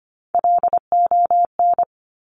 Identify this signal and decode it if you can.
Morse code, heard as a keyed tone, about 25 wpm, 700 Hz. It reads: LON
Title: Top Three Letter Combinations in Words 25wpm